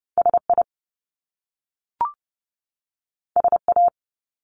Morse code, heard as a keyed tone, about 30 wpm, 700 Hz. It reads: SI SA
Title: Top Two Letter Combinations in Words 30wpm